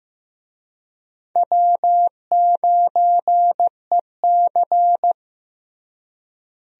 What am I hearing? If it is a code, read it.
Morse code, heard as a keyed tone, about 15 wpm, 700 Hz. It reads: W9EC
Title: USA Call Signs - 4 Characters - Repeated 2x - 15wpm